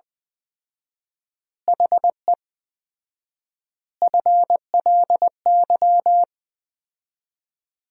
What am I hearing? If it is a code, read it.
Morse code, heard as a keyed tone, about 20 wpm, 700 Hz. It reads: HE FLY
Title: 4x Word-Spacing - Sentences from Top 500 Words - 20wpm